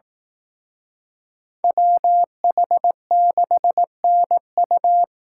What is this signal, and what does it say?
Morse code, heard as a keyed tone, about 18 wpm, 700 Hz. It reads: WH6NU
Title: Speed Racing - USA 2x2 Call Signs 15wpm